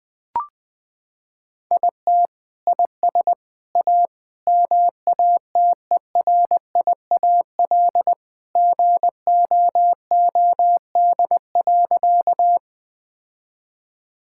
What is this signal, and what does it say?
Morse code, heard as a keyed tone, about 20 wpm, 700 Hz. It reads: IT IS A MATERIAL GOOD.
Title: Sentences from Top 1000 Words - Review Part 5 - 20wpm